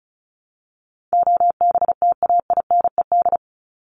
Morse code, heard as a keyed tone, about 34 wpm, 700 Hz. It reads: OBTAINED
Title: ED Suffix - Top 100 Words with Base Words 35wpm